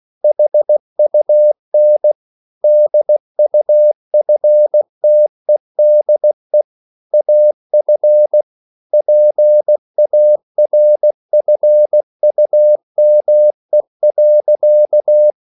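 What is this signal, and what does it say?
Morse code, heard as a keyed tone, about 16 wpm, 600 Hz. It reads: HUN DUFTEDE AF PARFUME.
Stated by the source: Slow (QRS) Morse code at 16 wpm, from CW med Gnister.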